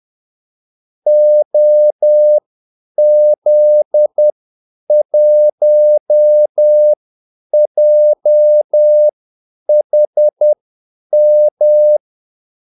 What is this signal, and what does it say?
Morse code, heard as a keyed tone, about 10 wpm, spaced as for 8 wpm, 600 Hz. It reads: OZ1JHM
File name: Callsigns-10-8wpm